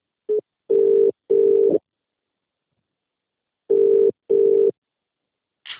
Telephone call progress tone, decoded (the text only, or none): ringback tone